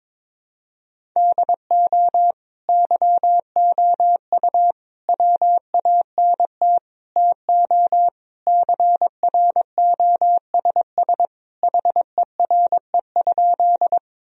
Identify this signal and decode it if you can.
Morse code, heard as a keyed tone, about 22 wpm, 700 Hz. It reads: DO YOU WANT TO CROSS HERE?
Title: Sentences from Top 1000 Words - Review Part 1 - 22wpm